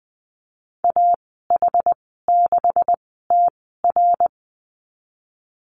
Morse code, heard as a keyed tone, about 20 wpm, spaced as for 15 wpm, 700 Hz. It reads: AH6TR
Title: 167 Lesson 27g - Number 7 - Call Signs 20wpm